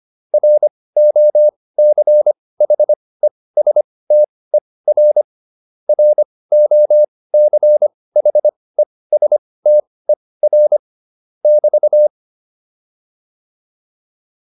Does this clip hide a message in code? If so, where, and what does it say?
Yes: Morse code from 0.3 to 12.1 s, heard as a keyed tone, about 25 wpm, spaced as for 18 wpm, 600 Hz. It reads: ROCHESTER ROCHESTER <BT>